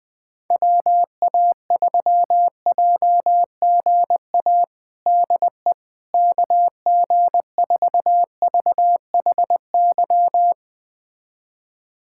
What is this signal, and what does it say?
Morse code, heard as a keyed tone, about 20 wpm, 700 Hz. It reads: WA3JGA DE KG4VHY
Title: USA Callsign DE Callsign 20wpm